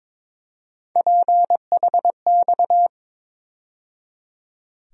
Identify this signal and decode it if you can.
Morse code, heard as a keyed tone, about 22 wpm, 700 Hz. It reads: PHX
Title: USA Call Sign Postfixes - 3 Letters 22wpm